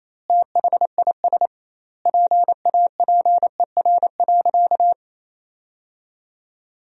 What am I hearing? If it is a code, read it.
Morse code, heard as a keyed tone, about 28 wpm, 700 Hz. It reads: THIS PAPER.
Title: 2x Word-Spacing - Sentences from Top 500 Words - 28wpm